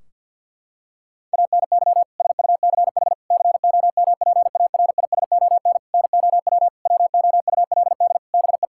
Morse code, heard as a keyed tone, about 48 wpm, 700 Hz. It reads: ANY SUCH COMPARISON NOW WOULD BE
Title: Sentences from Top 1600 Words 50wpm